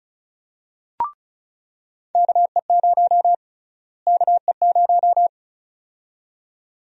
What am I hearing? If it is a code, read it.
Morse code, heard as a keyed tone, about 36 wpm, 700 Hz. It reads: KE0 KE0
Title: USA Call Sign Prefixes - Repeated 2x - 35wpm